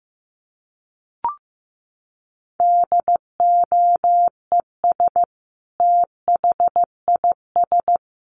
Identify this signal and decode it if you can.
Morse code, heard as a keyed tone, about 15 wpm, 700 Hz. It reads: DOES THIS
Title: A Sound Of Thunder 15wpm